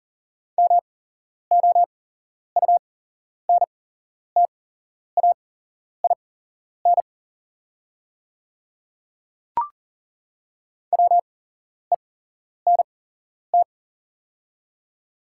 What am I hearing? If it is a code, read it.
Morse code, heard as a keyed tone, about 40 wpm, spaced as for 11 wpm, 700 Hz. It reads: MOUNTAIN WENT
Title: 8x Character Spacing - Top 500 Words - 40wpm